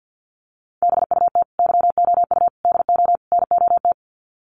Morse code, heard as a keyed tone, about 48 wpm, 700 Hz. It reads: BUT YOU DO NOT
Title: Sets of 4 Words 50wpm